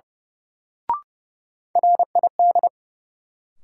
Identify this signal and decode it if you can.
Morse code, heard as a keyed tone, about 32 wpm, 700 Hz. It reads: RID
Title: Sets of 4 Words - New Word Drill - 30wpm